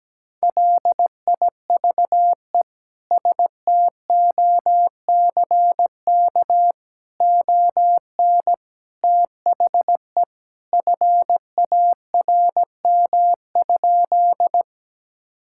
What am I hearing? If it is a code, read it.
Morse code, heard as a keyed tone, about 17 wpm, 700 Hz. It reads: LIVE STOCK ON THE FARM?
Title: Sentences from Top 1000 Words - Review Part 5 - 17wpm